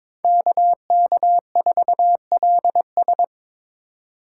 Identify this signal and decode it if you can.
Morse code, heard as a keyed tone, about 22 wpm, 700 Hz. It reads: KK4LS
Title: USA Call Signs - 5 Characters - 2x Character Spacing - 22wpm